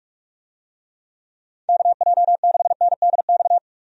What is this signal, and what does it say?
Morse code, heard as a keyed tone, about 46 wpm, 700 Hz. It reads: KJ6NDX
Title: USA Call Signs - 6 Characters - 45wpm